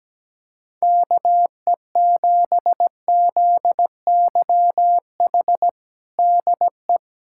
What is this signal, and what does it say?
Morse code, heard as a keyed tone, about 17 wpm, 700 Hz. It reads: KE7ZYH DE
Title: USA Callsign DE Callsign - 6 Characters - 17wpm